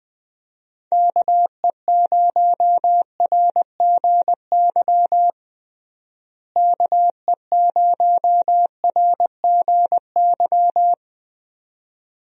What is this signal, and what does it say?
Morse code, heard as a keyed tone, about 20 wpm, 700 Hz. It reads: KE0RGY KE0RGY
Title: USA Call Signs - 6 Characters - Repeated 2x - 20wpm